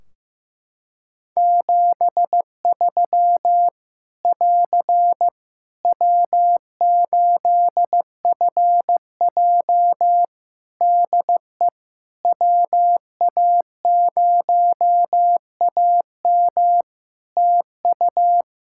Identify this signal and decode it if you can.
Morse code, heard as a keyed tone, about 15 wpm, 700 Hz. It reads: 73 <AR> W8FJ DE WA0AM TU
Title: Practice QSO - Exchange 4 - Ending - 15wpm